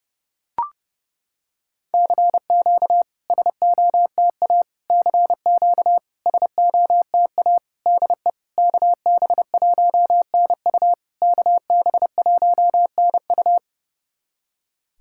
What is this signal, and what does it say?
Morse code, heard as a keyed tone, about 30 wpm, 690 Hz. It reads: CQ SOTA CQ SOTA DE KB1NU KB1NU
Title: SOTA QSO Practice 30wpm